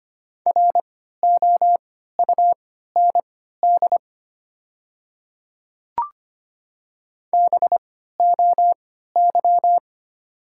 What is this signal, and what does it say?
Morse code, heard as a keyed tone, about 25 wpm, spaced as for 14 wpm, 700 Hz. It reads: ROUND BOY
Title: 3x Character Spacing - Top 500 Words - 25wpm